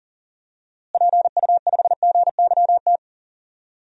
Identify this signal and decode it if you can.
Morse code, heard as a keyed tone, about 40 wpm, 700 Hz. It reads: PU5GYT
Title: DXCC-100 Call Signs - 6 Characters - Repeated 2x - 40wpm